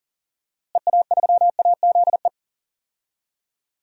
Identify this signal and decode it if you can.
Morse code, heard as a keyed tone, about 40 wpm, 700 Hz. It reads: EA3AZE
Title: DXCC-100 Call Signs - 6 Characters - Repeated 3x - 40wpm